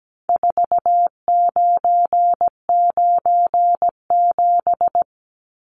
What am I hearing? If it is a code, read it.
Morse code, heard as a keyed tone, about 17 wpm, 700 Hz. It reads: 4997
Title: Numbers - 4-Digit 17wpm